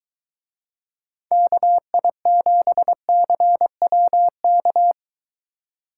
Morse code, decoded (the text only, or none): KI7CWK